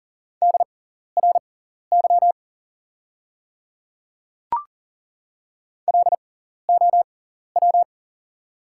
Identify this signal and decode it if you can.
Morse code, heard as a keyed tone, about 38 wpm, spaced as for 14 wpm, 690 Hz. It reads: DRY LOW
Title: 6x Character Spacing - Top 500 Words - 40wpm